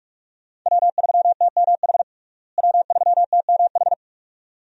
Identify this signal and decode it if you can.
Morse code, heard as a keyed tone, about 47 wpm, 700 Hz. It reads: W3TMH W3TMH
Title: USA Call Signs - 5 Characters - Repeated 2x - 45wpm